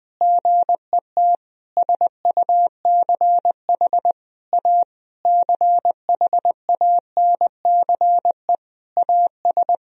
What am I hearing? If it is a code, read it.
Morse code, heard as a keyed tone, about 21 wpm, 700 Hz. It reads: GET SUCH A CHANCE AS
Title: Sentences from Top 1000 Words - Review Part 1 - 20wpm